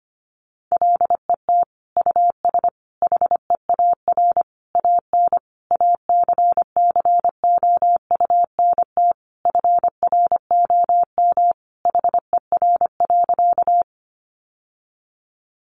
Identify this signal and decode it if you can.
Morse code, heard as a keyed tone, about 25 wpm, 700 Hz. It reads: LET US HEAR AN ACCOUNT FROM HER.
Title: Sentences from Top 900 Words 25wpm